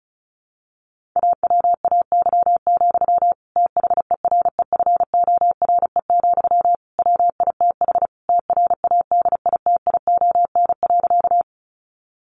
Morse code, heard as a keyed tone, about 34 wpm, 700 Hz. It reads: AWAY, THEREFORE, WITH TRADITION.
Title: Sentences from Top 1500 Words 35wpm